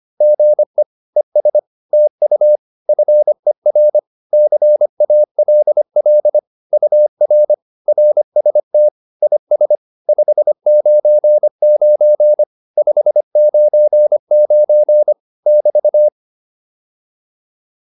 Morse code, heard as a keyed tone, about 25 wpm, 600 Hz. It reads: GE ES TU FER CALL UR RST IS 599 599 <BT>